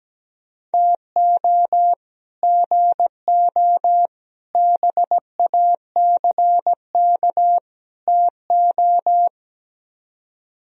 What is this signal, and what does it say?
Morse code, heard as a keyed tone, about 17 wpm, 700 Hz. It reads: TO GO BACK TO
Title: Speed-Racing - Sets of 4 Words 17wpm